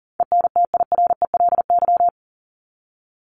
Morse code, heard as a keyed tone, about 40 wpm, 700 Hz. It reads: ENTIRELY
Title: LY Suffix - Top 100 Words with Base Words 40wpm